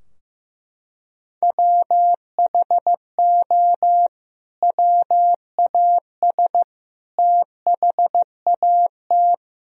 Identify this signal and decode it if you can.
Morse code, heard as a keyed tone, about 15 wpm, 700 Hz. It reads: WHO WAS THAT
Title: Sentences from Top 1000 Words - Review Part 4 - 15wpm